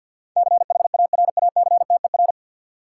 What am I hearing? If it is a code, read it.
Morse code, heard as a keyed tone, about 52 wpm, 700 Hz. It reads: CHARACTER
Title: Sentences from Top 1200 Words - New Word Drill - 50wpm